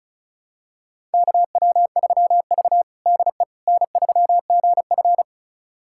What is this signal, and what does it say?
Morse code, heard as a keyed tone, about 36 wpm, 700 Hz. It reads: KW3V DE N3GF
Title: USA Callsign DE Callsign - 4 Characters - 35wpm